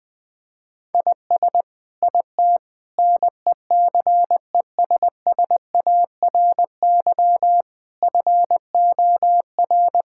Morse code, heard as a keyed tone, about 20 wpm, 700 Hz. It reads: IS IT NECESSARY FOR
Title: Sentences from Top 1000 Words - Review Part 6 - 20wpm